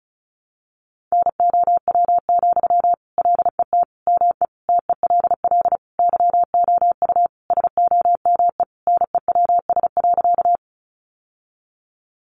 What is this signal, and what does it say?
Morse code, heard as a keyed tone, about 34 wpm, 700 Hz. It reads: NOW, LET ME TELL YOU SOME NEWS.
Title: Sentences from Top 800 Words 35wpm